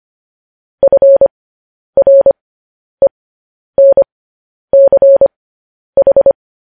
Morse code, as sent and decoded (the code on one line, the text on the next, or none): ..-. .-. -. -.-. ....
FRENCH